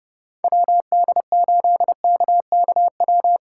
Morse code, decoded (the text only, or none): WD8KKW